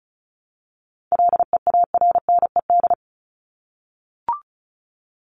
Morse code, heard as a keyed tone, about 34 wpm, 700 Hz. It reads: LEARNED
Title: ED Suffix - Top 100 Words with Base Words 35wpm